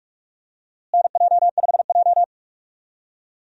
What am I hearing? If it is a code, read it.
Morse code, heard as a keyed tone, about 46 wpm, 700 Hz. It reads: NJ5J
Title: USA Call Signs - 4 Characters - Repeated 3x - 45wpm